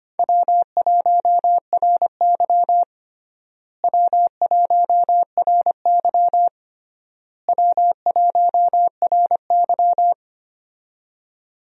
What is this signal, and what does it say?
Morse code, heard as a keyed tone, about 25 wpm, 700 Hz. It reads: W1RY W1RY W1RY
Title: USA Call Signs - 4 Characters - Repeated 3x - 25wpm